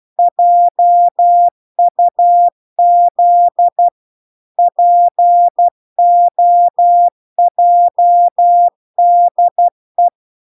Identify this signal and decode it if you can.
Morse code, heard as a keyed tone, about 12 wpm, 700 Hz. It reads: JUZ POJDE